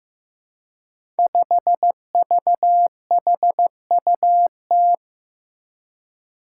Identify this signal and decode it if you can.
Morse code, heard as a keyed tone, about 15 wpm, 700 Hz. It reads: 5VHUT
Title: Five Challenging Letter-Number Combinations 15wpm